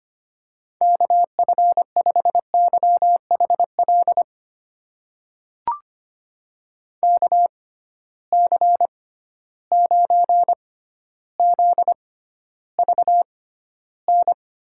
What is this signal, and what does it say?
Morse code, heard as a keyed tone, about 25 wpm, spaced as for 13 wpm, 700 Hz. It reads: KF5YHL KC9ZVN
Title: USA Call Signs - 6 Characters - 6x Character Spacing - 25wpm